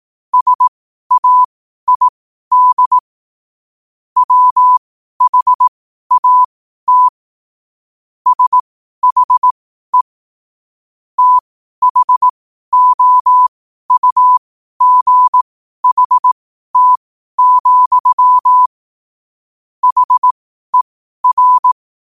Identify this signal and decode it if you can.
Morse code, heard as a keyed tone, about 18 wpm, spaced as for 12 wpm, 1000 Hz. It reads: SAID WHAT SHE THOUGHT, HER
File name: qotdfiles / 12wpm